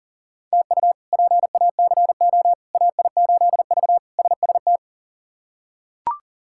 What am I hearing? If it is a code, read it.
Morse code, heard as a keyed tone, about 40 wpm, 700 Hz. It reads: TU PACO AI8V SST
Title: K1USN SST QSO Practice - 40wpm